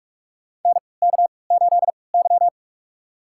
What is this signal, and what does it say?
Morse code, heard as a keyed tone, about 46 wpm, spaced as for 25 wpm, 690 Hz. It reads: NK8Y
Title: USA Call Signs - 4 Characters - 3x Character Spacing - 45wpm